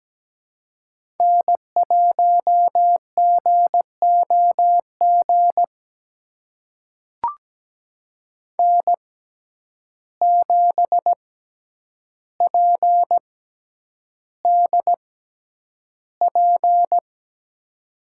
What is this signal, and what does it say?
Morse code, heard as a keyed tone, about 17 wpm, spaced as for 9 wpm, 700 Hz. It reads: N1GOG N7PDP
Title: USA Call Signs - 5 Characters - 6x Character Spacing - 17wpm